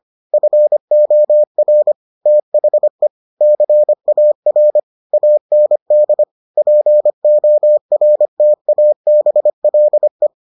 Morse code, heard as a keyed tone, about 25 wpm, 600 Hz. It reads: FOR THE CAR AND PORTABLE